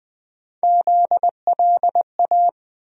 Morse code, heard as a keyed tone, about 20 wpm, 700 Hz. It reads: ZLA
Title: USA Call Sign Postfixes - 3 Letters 20wpm